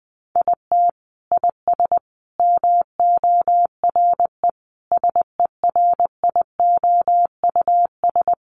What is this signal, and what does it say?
Morse code, heard as a keyed tone, about 20 wpm, 700 Hz. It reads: IT IS MORE SERIOUS
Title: Sentences from Top 1100 Words 20wpm